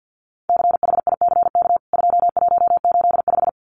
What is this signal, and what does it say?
Morse code, heard as a keyed tone, about 50 wpm, 700 Hz. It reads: CHICK 2185
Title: CWT Exchange 50wpm